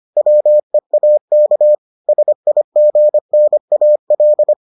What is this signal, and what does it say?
Morse code, heard as a keyed tone, about 25 wpm, 600 Hz. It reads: WEAK SIGNAL